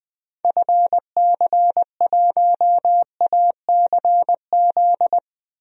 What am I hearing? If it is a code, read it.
Morse code, heard as a keyed tone, about 20 wpm, 700 Hz. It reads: FC1ACZ
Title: DXCC-100 Call Signs - 6 Characters - Repeated 2x - 20wpm